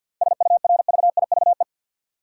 Morse code, heard as a keyed tone, about 53 wpm, 700 Hz. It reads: SURVIVE
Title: Sentences from Top 1300 Words - New Word Drill Alt - 50wpm